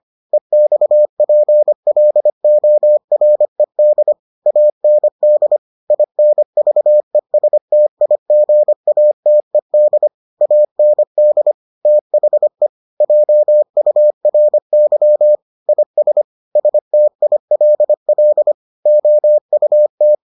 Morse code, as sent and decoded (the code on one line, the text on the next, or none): . -..- .--. .-.. --- .-. -.. / .- -. -.. / .. -. ...- . ... - .. --. .- - . -.. / .- -. -.. / - .... / .--- ..- .-. -.-- / .. ... / ... - .. .-.. .-.. / --- ..- -
EXPLORED AND INVESTIGATED AND THE JURY IS STILL OUT